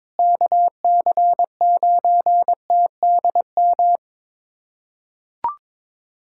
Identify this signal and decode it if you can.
Morse code, heard as a keyed tone, about 22 wpm, 700 Hz. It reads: KC9TDM
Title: USA Callsign DE Callsign - 6 Characters - 22wpm